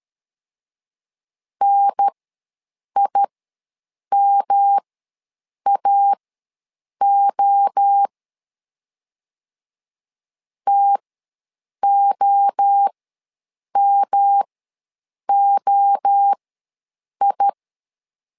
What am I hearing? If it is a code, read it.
Morse code, heard as a keyed tone, about 13 wpm, spaced as for 7 wpm, 800 Hz. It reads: NIMAO TOMOI
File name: LEZIONI / 3 MWRO / 3 MWRO ws 12wpm 1914